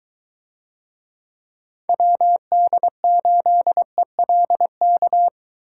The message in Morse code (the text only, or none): WD8ELK